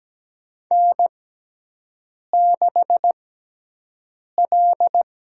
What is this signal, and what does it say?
Morse code, heard as a keyed tone, about 17 wpm, spaced as for 6 wpm, 700 Hz. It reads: N6L
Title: USA Call Signs - 6x Character Spacing - 17wpm